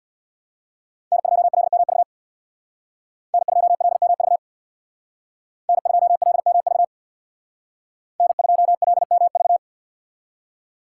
Morse code, heard as a keyed tone, about 63 wpm, 700 Hz. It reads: N2LMV N2LMV N2LMV N2LMV